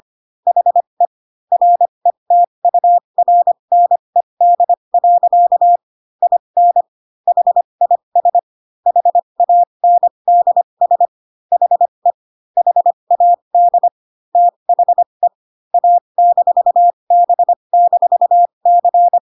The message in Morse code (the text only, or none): HE RETURNED. IN HIS HANDS HE HAD THE A-B-C